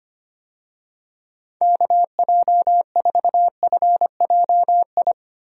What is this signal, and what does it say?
Morse code, heard as a keyed tone, about 25 wpm, 700 Hz. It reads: KJ4FJI
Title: USA Call Signs - 6 Characters - Repeated 2x - 25wpm